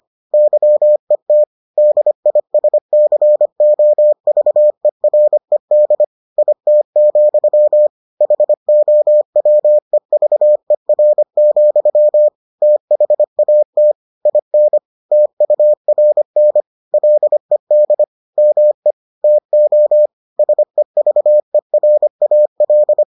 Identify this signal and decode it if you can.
Morse code, heard as a keyed tone, about 25 wpm, 600 Hz. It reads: YET DISCOVERED IT, HOWEVER, THAT IN TURN LED ME TO SEVERAL